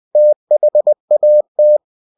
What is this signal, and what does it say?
Morse code, heard as a keyed tone, about 20 wpm, 600 Hz. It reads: THAT
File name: cw-books / winnie-the-pooh / winnie-20wpm-page-0004